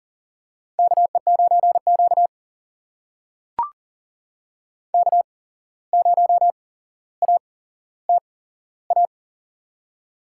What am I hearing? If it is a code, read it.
Morse code, heard as a keyed tone, about 40 wpm, spaced as for 16 wpm, 700 Hz. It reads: KE9Q K0ATA
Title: USA Call Signs - 8x Character Spacing - 40wpm